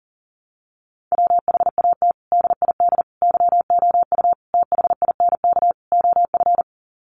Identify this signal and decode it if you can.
Morse code, heard as a keyed tone, about 40 wpm, 700 Hz. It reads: WHAT DID YOU THINK OF
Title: Sets of 5 Words Encore2 40wpm